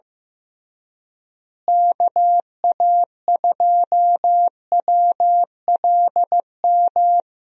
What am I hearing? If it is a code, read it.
Morse code, heard as a keyed tone, about 15 wpm, 700 Hz. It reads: KA2WLM
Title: USA Call Signs - 6 Characters - Repeated 3x - 15wpm